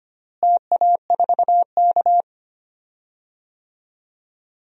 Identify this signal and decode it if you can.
Morse code, heard as a keyed tone, about 25 wpm, 700 Hz. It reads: TA4K
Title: DXCC-100 Call Signs - 4 Characters - Repeated 2x - 25wpm